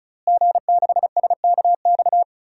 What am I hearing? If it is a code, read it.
Morse code, heard as a keyed tone, about 34 wpm, 700 Hz. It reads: G6SKX
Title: DXCC-100 Call Signs - 5 Characters - Repeated 2x - 35wpm